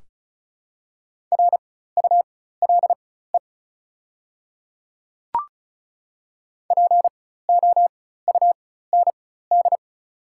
Morse code, heard as a keyed tone, about 36 wpm, spaced as for 16 wpm, 690 Hz. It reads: RULE POUND
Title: 4x Character Spacing - Top 500 Words - 35wpm